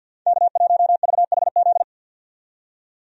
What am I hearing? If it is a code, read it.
Morse code, heard as a keyed tone, about 52 wpm, 700 Hz. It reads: K1VHZ
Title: USA Call Signs - 5 Characters - 50wpm